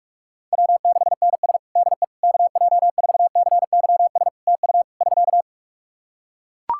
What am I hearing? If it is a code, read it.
Morse code, heard as a keyed tone, about 42 wpm, 700 Hz. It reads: W6NS DE KJ4CYS TU <SK>
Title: Practice QSO - Exchange 4 - Ending - 45wpm